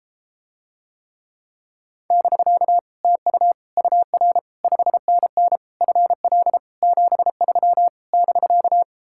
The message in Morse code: <BK> TU UR 5NN FL 73 <BK>

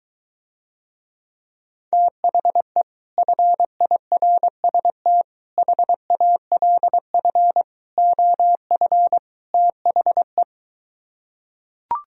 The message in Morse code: THE FIRST HALF OF THE

